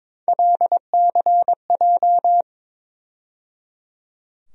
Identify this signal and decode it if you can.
Morse code, heard as a keyed tone, about 22 wpm, 700 Hz. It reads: LCJ
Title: USA Call Sign Postfixes - 3 Letters 22wpm